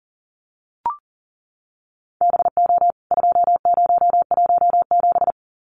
Morse code, heard as a keyed tone, about 40 wpm, 700 Hz. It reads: BO 2017
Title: CWT Exchange 40wpm